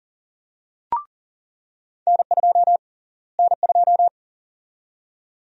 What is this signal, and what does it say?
Morse code, heard as a keyed tone, about 40 wpm, 700 Hz. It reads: N2 N2
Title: USA Call Sign Prefixes - Repeated 2x - 40wpm